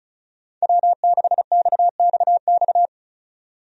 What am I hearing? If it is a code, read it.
Morse code, heard as a keyed tone, about 36 wpm, 700 Hz. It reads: W6XXX
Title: USA Call Signs - 5 Characters - 35wpm